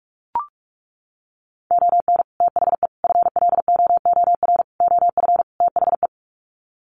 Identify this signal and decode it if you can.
Morse code, heard as a keyed tone, about 44 wpm, 700 Hz. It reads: ON THE FLOOR OF THE
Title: Sets of 5 Words 45wpm